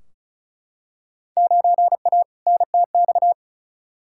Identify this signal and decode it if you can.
Morse code, heard as a keyed tone, about 36 wpm, 700 Hz. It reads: 9A NTX
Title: Field Day Exchanges Encore - 35wpm